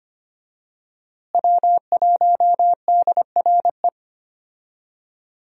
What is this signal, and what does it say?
Morse code, heard as a keyed tone, about 25 wpm, 700 Hz. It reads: W1DRE
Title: USA Call Signs - 5 Characters - Repeated 2x - 25wpm